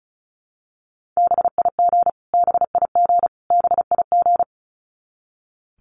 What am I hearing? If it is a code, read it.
Morse code, heard as a keyed tone, about 34 wpm, 700 Hz. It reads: BIG BIG BIG
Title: Top 200 Words Repeated 35wpm